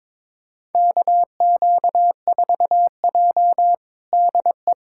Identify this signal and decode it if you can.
Morse code, heard as a keyed tone, about 22 wpm, 700 Hz. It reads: KQ4J DE
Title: USA Callsign DE Callsign - 4 Characters - 22wpm